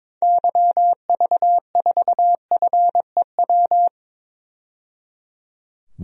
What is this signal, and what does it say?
Morse code, heard as a keyed tone, about 22 wpm, 700 Hz. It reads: YV4FEW
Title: DXCC-100 Call Signs - 6 Characters - 22wpm